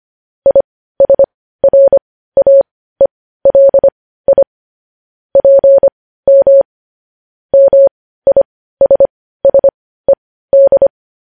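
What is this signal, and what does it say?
Morse code, heard as a keyed tone, about 25 wpm, spaced as for 15 wpm, 550 Hz. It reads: ISRAELI PM MISSED
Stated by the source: Morse code news headlines broadcast.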